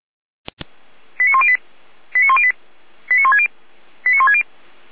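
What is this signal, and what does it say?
Five-tone selective call (ZVEI), heard as quick runs of five tones, tones of about 70 ms. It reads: 89108 89108 89160 89160
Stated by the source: ZVEI five-tone selective call received by SDR, logged on the Feuerwehr Orlamünde fire brigade server.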